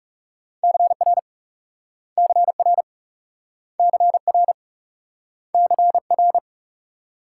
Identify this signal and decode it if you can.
Morse code, heard as a keyed tone, about 38 wpm, 700 Hz. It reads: CR CR CR CR